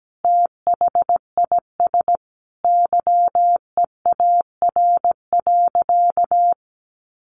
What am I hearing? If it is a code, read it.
Morse code, heard as a keyed tone, about 17 wpm, 700 Hz. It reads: THIS YEAR.
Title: Sentences from Top 1400 Words 17wpm